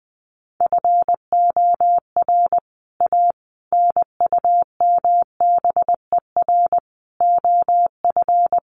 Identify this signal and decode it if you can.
Morse code, heard as a keyed tone, about 20 wpm, 700 Hz. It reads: FOR A NUMBER OF
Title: Sets of 5 Words Encore2 20wpm